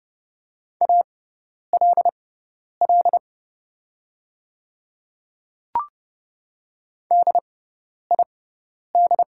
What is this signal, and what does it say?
Morse code, heard as a keyed tone, about 29 wpm, spaced as for 10 wpm, 690 Hz. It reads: ALL DID
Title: 6x Character Spacing - Top 100 Words - 30wpm